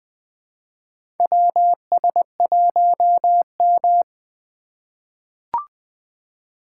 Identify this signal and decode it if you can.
Morse code, heard as a keyed tone, about 20 wpm, 700 Hz. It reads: WS1M
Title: USA Call Signs - 4 Characters - 3x Character Spacing - 20wpm